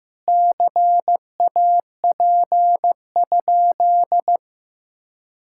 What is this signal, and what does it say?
Morse code, heard as a keyed tone, about 15 wpm, 700 Hz. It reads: CAP?
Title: Sentences from Top 1600 Words 15wpm